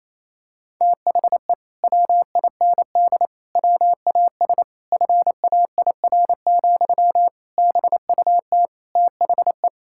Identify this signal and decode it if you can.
Morse code, heard as a keyed tone, about 28 wpm, 700 Hz. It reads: THE WIND WAS FAIR, BUT THE
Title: Sentences from Top 1600 Words 28wpm